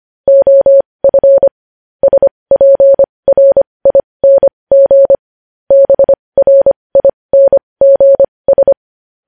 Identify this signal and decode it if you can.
Morse code, heard as a keyed tone, about 25 wpm, spaced as for 20 wpm, 550 Hz. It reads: OF SPRING BRINGS
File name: podcast / News headlines-1x-25x20